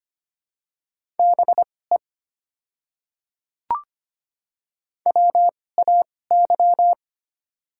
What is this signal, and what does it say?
Morse code, heard as a keyed tone, about 25 wpm, spaced as for 18 wpm, 700 Hz. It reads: BE WAY
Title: 2x Character Spacing - Top 100 Words - 25wpm